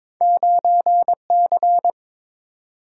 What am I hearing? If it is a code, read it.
Morse code, heard as a keyed tone, about 22 wpm, 700 Hz. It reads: 9C
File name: practice / 22wpm / Two Letter-Number Combinations - 22wpm